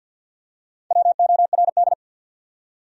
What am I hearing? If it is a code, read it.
Morse code, heard as a keyed tone, about 52 wpm, 700 Hz. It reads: WORD
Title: Top 100 Words Repeated 50wpm